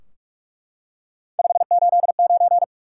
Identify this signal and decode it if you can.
Morse code, heard as a keyed tone, about 44 wpm, 700 Hz. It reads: 589
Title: Top 180 QSO Elements 45wpm